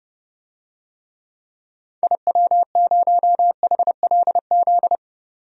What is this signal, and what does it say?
Morse code, heard as a keyed tone, about 30 wpm, 700 Hz. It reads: IW0HLZ
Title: DXCC-100 Call Signs - 6 Characters - 30wpm